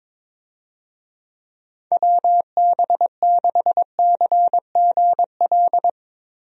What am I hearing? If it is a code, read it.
Morse code, heard as a keyed tone, about 21 wpm, 700 Hz. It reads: WB6CGL
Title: USA Call Signs - 6 Characters - Repeated 2x - 22wpm